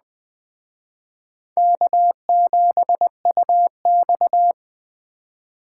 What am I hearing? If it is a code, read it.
Morse code, heard as a keyed tone, about 20 wpm, 700 Hz. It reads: K7UX